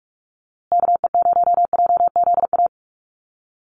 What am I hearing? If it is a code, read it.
Morse code, heard as a keyed tone, about 44 wpm, 700 Hz. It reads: KE0JZA
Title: USA Call Signs Repeated 2x 45wpm